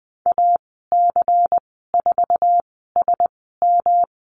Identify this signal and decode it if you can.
Morse code, heard as a keyed tone, about 20 wpm, spaced as for 15 wpm, 700 Hz. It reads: AC4SM
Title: 128 Lesson 21g - Number 6 - Call Signs 20wpm